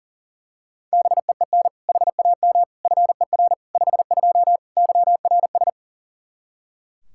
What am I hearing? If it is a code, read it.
Morse code, heard as a keyed tone, about 37 wpm, 700 Hz. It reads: BEEN HAM FER 52 YRS
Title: Practice QSO - Exchange 3 - More Info - 40wpm